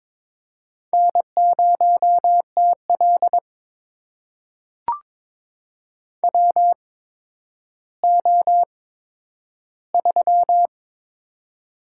Morse code, heard as a keyed tone, about 22 wpm, spaced as for 11 wpm, 700 Hz. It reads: N0TL WO3
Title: USA Call Signs - 4 Characters - 8x Character Spacing - 22wpm